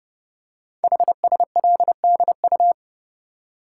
Morse code, heard as a keyed tone, about 30 wpm, 700 Hz. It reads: HSLDU
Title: Code Groups - 5 Letter:Numbers - 30wpm